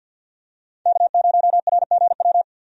Morse code, heard as a keyed tone, about 50 wpm, 700 Hz. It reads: K0RGW
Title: USA Call Signs - 5 Characters - 50wpm